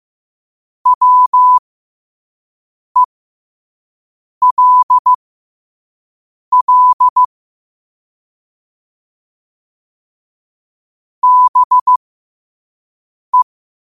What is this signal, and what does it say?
Morse code, heard as a keyed tone, about 15 wpm, spaced as for 5 wpm, 1000 Hz. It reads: WELL BE
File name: qotdfiles / 5wpm